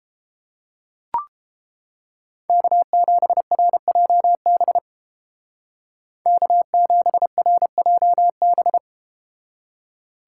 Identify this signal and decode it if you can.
Morse code, heard as a keyed tone, about 31 wpm, 700 Hz. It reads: K7RJB K7RJB